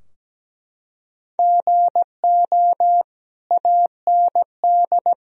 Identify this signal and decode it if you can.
Morse code, heard as a keyed tone, about 17 wpm, 700 Hz. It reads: GO AND
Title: Sentences from Top 1000 Words - Review Part 4 - 17wpm